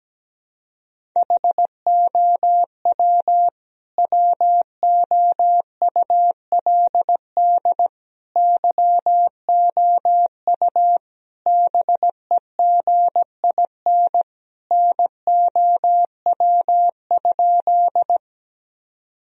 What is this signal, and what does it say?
Morse code, heard as a keyed tone, about 17 wpm, 700 Hz. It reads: HOW WOULD YOU BEGIN NOW?
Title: Sentences from Top 1000 Words - Review Part 1 - 17wpm